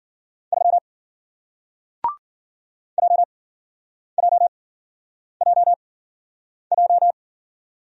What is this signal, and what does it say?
Morse code, heard as a keyed tone, about 53 wpm, spaced as for 9 wpm, 700 Hz. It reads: 3 JJJJ